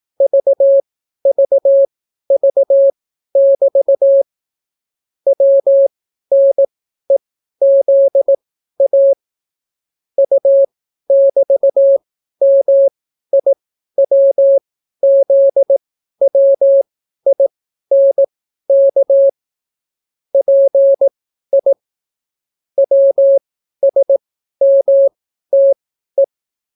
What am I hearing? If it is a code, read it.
Morse code, heard as a keyed tone, about 18 wpm, spaced as for 12 wpm, 550 Hz. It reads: VVV<BT> WNEZA U<BT>MIWZWINK PI WSMTE